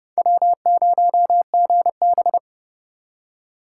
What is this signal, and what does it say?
Morse code, heard as a keyed tone, about 39 wpm, 700 Hz. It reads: W0GB